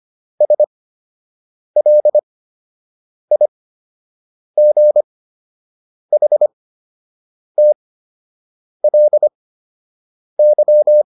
Morse code, heard as a keyed tone, about 25 wpm, spaced as for 7 wpm, 600 Hz. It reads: SLIGHTLY